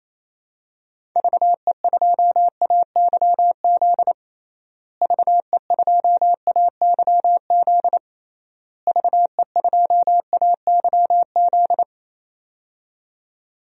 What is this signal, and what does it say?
Morse code, heard as a keyed tone, about 28 wpm, 700 Hz. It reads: VE2AYZ VE2AYZ VE2AYZ
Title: DXCC-100 Call Signs - 6 Characters - Repeated 3x - 28wpm